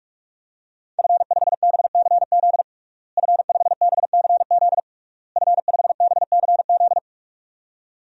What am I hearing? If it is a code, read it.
Morse code, heard as a keyed tone, about 46 wpm, 700 Hz. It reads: F5BCZ F5BCZ F5BCZ
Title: DXCC-100 Call Signs - 5 Characters - Repeated 3x - 45wpm